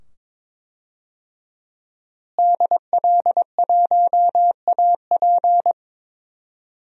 Morse code, heard as a keyed tone, about 22 wpm, 700 Hz. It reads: DL1AP
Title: DXCC-100 Call Signs - 5 Characters - Repeated 2x - 22wpm